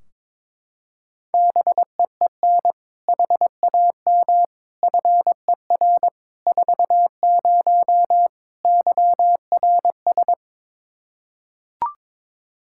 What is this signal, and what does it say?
Morse code, heard as a keyed tone, about 22 wpm, 700 Hz. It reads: BEEN HAM FER 40 YRS
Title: Practice QSO - Exchange 3 - More Info - 22wpm